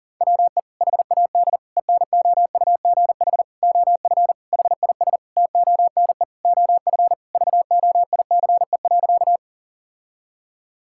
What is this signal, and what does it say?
Morse code, heard as a keyed tone, about 40 wpm, 700 Hz. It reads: WE HAD ENOUGH OF HIS TONE OF VOICE.
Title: Sentences from Top 1600 Words 40wpm